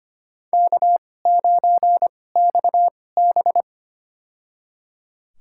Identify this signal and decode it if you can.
Morse code, heard as a keyed tone, about 25 wpm, spaced as for 18 wpm, 700 Hz. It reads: K9XB
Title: USA Call Signs - 2x Character Spacing - 25wpm